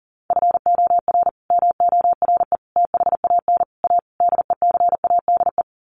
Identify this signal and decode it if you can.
Morse code, heard as a keyed tone, about 40 wpm, 700 Hz. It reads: FOR MORE THAN A DECADE
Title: Sets of 5 Words 40wpm